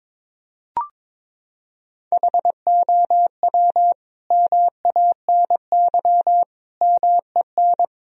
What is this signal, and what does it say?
Morse code, heard as a keyed tone, about 21 wpm, 700 Hz. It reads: HOW MANY MEN
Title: Sentences from Top 1000 Words - Review Part 3 - 22wpm